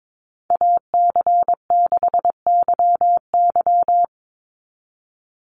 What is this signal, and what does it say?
Morse code, heard as a keyed tone, about 22 wpm, 700 Hz. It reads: AC6YY
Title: Call Signs Repeated 2x - 22wpm